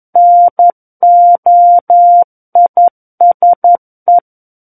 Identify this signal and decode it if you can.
Morse code, heard as a keyed tone, about 11 wpm, 700 Hz. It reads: NOISE
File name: Book One / 02-Chapter 2 - 11 WPM